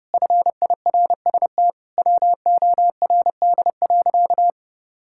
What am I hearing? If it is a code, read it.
Morse code, heard as a keyed tone, about 32 wpm, 700 Hz. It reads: FIRST WORD.
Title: Pangram Challenge v2 - 30wpm